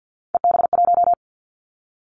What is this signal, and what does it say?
Morse code, heard as a keyed tone, about 50 wpm, 700 Hz. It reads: EB1
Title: Callsign Prefixes for Top 25 DXCC - 50wpm